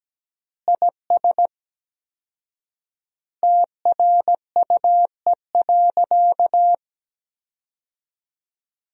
Morse code, heard as a keyed tone, about 17 wpm, 700 Hz. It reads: IS TRUE.
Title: 4x Word-Spacing - Sentences from Top 400 Words - 17wpm